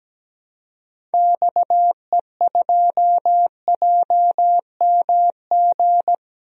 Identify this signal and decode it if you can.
Morse code, heard as a keyed tone, about 17 wpm, 700 Hz. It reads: XE2JMG
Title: DXCC-100 Call Signs - 6 Characters - Repeated 2x - 17wpm